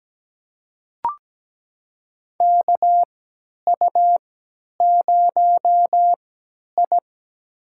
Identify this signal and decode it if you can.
Morse code, heard as a keyed tone, about 17 wpm, spaced as for 10 wpm, 700 Hz. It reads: KU0I
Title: USA Call Signs - 3x Character Spacing - 17wpm